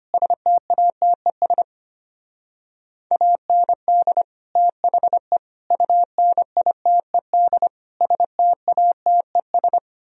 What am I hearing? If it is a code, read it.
Morse code, heard as a keyed tone, about 30 wpm, 700 Hz. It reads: STATES AND THE UNITED STATES
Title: Speed-Racing - Sets of 4 Words 25wpm